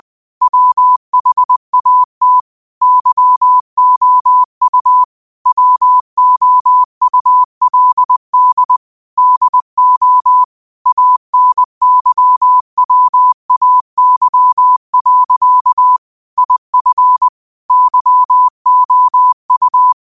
Morse code, heard as a keyed tone, about 20 wpm, 1000 Hz. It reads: WHAT YOU WOULD DO ANYWAY. IF YOU